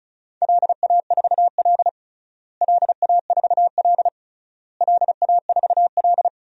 Morse code, heard as a keyed tone, about 36 wpm, 700 Hz. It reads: LA4L LA4L LA4L
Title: DXCC-100 Call Signs - 4 Characters - Repeated 3x - 35wpm